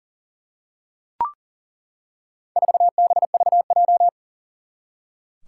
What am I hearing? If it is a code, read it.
Morse code, heard as a keyed tone, about 40 wpm, 700 Hz. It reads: 4BVJ
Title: Four Challenging Letter-Number Combinations 40wpm